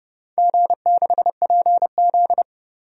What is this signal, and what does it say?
Morse code, heard as a keyed tone, about 29 wpm, 700 Hz. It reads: G6PZ
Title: DXCC-100 Call Signs - 4 Characters - 30wpm